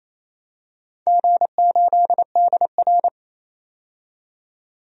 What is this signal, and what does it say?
Morse code, heard as a keyed tone, about 28 wpm, 700 Hz. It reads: G8DR
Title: DXCC-100 Call Signs - 4 Characters - 28wpm